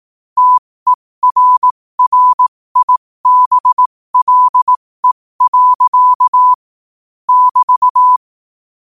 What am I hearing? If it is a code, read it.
Morse code, heard as a keyed tone, about 18 wpm, spaced as for 15 wpm, 1000 Hz. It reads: TERRIBLE. <BT>
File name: qotdfiles / 15wpm